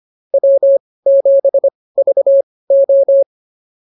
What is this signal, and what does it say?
Morse code, heard as a keyed tone, about 25 wpm, spaced as for 18 wpm, 550 Hz. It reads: W7VO